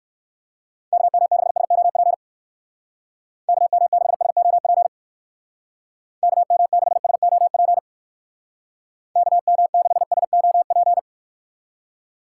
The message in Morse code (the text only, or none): KM6SOP KM6SOP KM6SOP KM6SOP